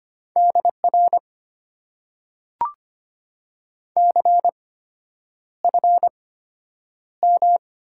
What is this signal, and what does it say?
Morse code, heard as a keyed tone, about 25 wpm, spaced as for 9 wpm, 700 Hz. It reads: DR CFM
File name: Top 90 QSO Elements - 8x Character Spacing - 25wpm